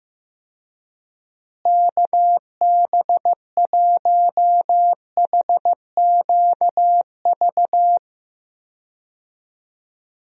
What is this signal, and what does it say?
Morse code, heard as a keyed tone, about 15 wpm, 700 Hz. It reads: KB1HQV
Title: USA Call Signs - 6 Characters - Repeated 2x - 15wpm